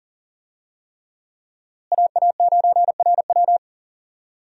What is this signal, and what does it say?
Morse code, heard as a keyed tone, about 39 wpm, 700 Hz. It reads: AA9RW
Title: USA Call Signs - 5 Characters - Repeated 3x - 40wpm